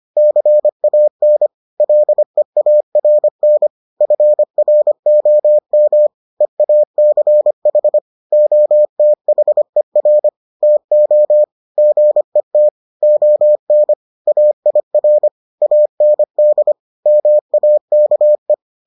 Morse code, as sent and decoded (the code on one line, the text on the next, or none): -.-. .- -. / .-.. .- .-. -. / ..-. .-. --- -- / . .- -.-. .... / --- - .... .-. / - --- / --. - / --- -. / .- .. .-. / .- -. -.. / -- .- -.- .
CAN LEARN FROM EACH OTHER TO GET ON AIR AND MAKE